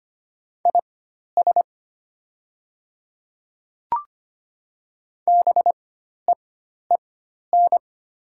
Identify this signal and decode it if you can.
Morse code, heard as a keyed tone, about 25 wpm, spaced as for 12 wpm, 700 Hz. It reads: IS BEEN
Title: 4x Character Spacing - Top 200 Words - 25wpm